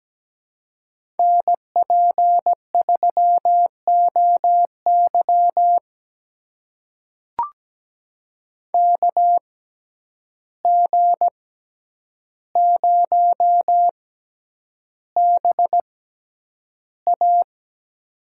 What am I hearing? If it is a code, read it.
Morse code, heard as a keyed tone, about 17 wpm, spaced as for 9 wpm, 700 Hz. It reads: NP3OY KG0BA
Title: USA Call Signs - 5 Characters - 6x Character Spacing - 17wpm